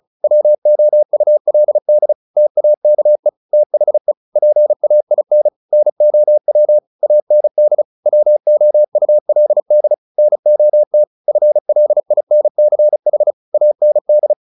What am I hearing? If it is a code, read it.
Morse code, heard as a keyed tone, about 35 wpm, 600 Hz. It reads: WOULD TAKE THE PAIN NOW AND WOULD NOT FLINCH AND
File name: the old man and the sea 2 35wpm